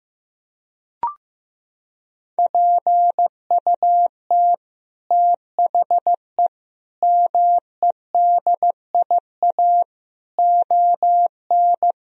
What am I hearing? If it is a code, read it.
Morse code, heard as a keyed tone, about 15 wpm, 700 Hz. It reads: PUT THE MEDIA ON
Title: Sentences from Top 1000 Words - Review Part 4 - 15wpm